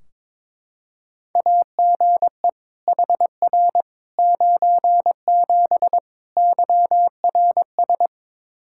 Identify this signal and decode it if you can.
Morse code, heard as a keyed tone, about 22 wpm, 700 Hz. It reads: AGE HR 97 YRS
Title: Practice QSO - Exchange 3 - More Info - 22wpm